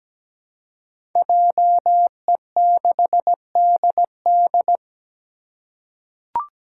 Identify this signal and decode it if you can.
Morse code, heard as a keyed tone, about 17 wpm, 700 Hz. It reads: JE6DD
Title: Five Challenging Letter-Number Combinations 17wpm